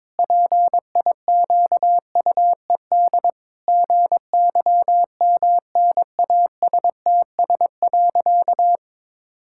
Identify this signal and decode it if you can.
Morse code, heard as a keyed tone, about 22 wpm, 700 Hz. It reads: PIQUED GYMNASTS.
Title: Pangram Challenge v2 - 22wpm